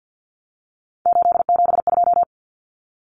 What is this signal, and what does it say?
Morse code, heard as a keyed tone, about 48 wpm, 700 Hz. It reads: 872
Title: Common Cut Numbers - 3-Digit 50wpm